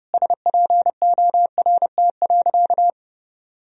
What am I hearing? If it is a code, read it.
Morse code, heard as a keyed tone, about 32 wpm, 700 Hz. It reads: SPORT.
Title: Sentences from Top 1000 Words - Review Part 4 - 30wpm